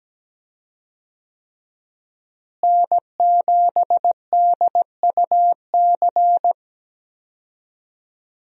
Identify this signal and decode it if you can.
Morse code, heard as a keyed tone, about 17 wpm, 700 Hz. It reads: N7DUC
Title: USA Call Signs - 5 Characters - Repeated 3x - 17wpm